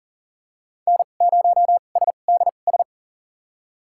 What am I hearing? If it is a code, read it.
Morse code, heard as a keyed tone, about 40 wpm, spaced as for 29 wpm, 700 Hz. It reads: N0SDS
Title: USA Call Signs - 5 Characters - 2x Character Spacing - 40wpm